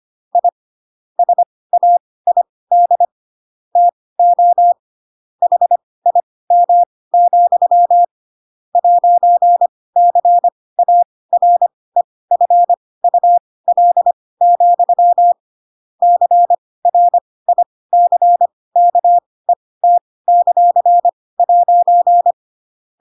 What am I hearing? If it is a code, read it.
Morse code, heard as a keyed tone, about 25 wpm, spaced as for 18 wpm, 700 Hz. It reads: I SAID TO HIM, 'CAREFUL, CRICKET;'
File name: Pinocchio / Pinocchio ch7 12 25 ik1hge